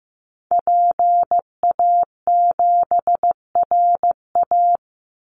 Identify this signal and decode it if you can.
Morse code, heard as a keyed tone, about 15 wpm, 700 Hz. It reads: PA7RA
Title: Call Signs Repeated 3x - 15wpm